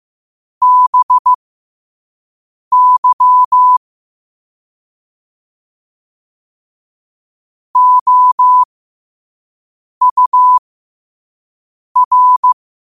Morse code, heard as a keyed tone, about 15 wpm, spaced as for 5 wpm, 1000 Hz. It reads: BY OUR
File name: qotdfiles / 5wpm